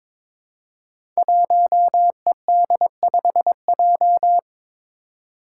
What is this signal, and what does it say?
Morse code, heard as a keyed tone, about 22 wpm, 700 Hz. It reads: 1ED5J
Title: Five Challenging Letter-Number Combinations 22wpm